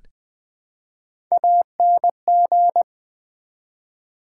Morse code, heard as a keyed tone, about 20 wpm, 700 Hz. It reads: ANG
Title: Build A Word - 1 Letter At A Time - 3 to 5-Letter Words - 20wpm